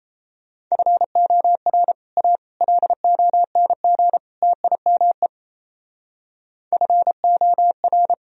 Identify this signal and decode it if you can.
Morse code, heard as a keyed tone, about 35 wpm, 700 Hz. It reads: FOR A LONG TIME FOR
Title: Speed-Racing - Sets of 4 Words 28wpm